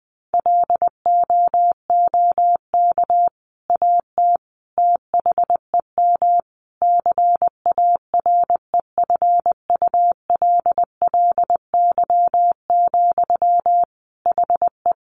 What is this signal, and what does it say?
Morse code, heard as a keyed tone, about 20 wpm, 700 Hz. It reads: LOOK AT THEM CAREFULLY, HE
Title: Sentences from Top 1300 Words 20wpm